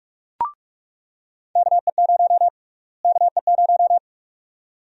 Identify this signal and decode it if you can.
Morse code, heard as a keyed tone, about 41 wpm, 700 Hz. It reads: KE0 KE0
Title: USA Call Sign Prefixes - Repeated 2x - 45wpm